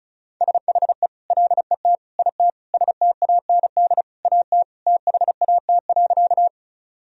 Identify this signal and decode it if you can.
Morse code, heard as a keyed tone, about 36 wpm, 690 Hz. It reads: SHE LET IT STAND AT THAT.
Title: Sentences from Top 1000 Words - Review Part 1 - 35wpm